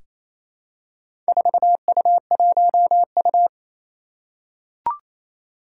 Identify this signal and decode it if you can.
Morse code, heard as a keyed tone, about 27 wpm, 700 Hz. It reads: 4U1U
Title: Four Challenging Letter-Number Combinations 28wpm